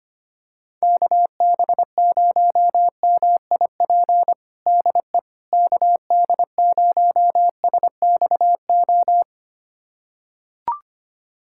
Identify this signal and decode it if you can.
Morse code, heard as a keyed tone, about 25 wpm, 700 Hz. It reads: KB0MIP DE KD0SXO
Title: USA Callsign DE Callsign 25wpm